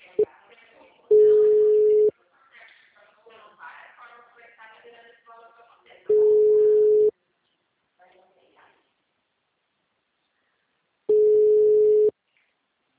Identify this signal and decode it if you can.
Ringback tone (Europe, 425 Hz), with speech.
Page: Aracatuba - SP